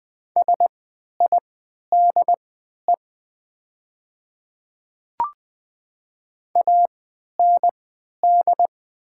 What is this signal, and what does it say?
Morse code, heard as a keyed tone, about 20 wpm, spaced as for 11 wpm, 700 Hz. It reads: SIDE AND
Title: 3x Character Spacing - Top 100 Words - 20wpm